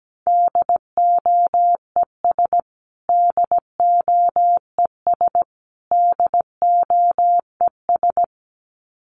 Morse code, heard as a keyed tone, about 17 wpm, 700 Hz. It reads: DOES DOES DOES
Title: Top 500 Words Repeated 17wpm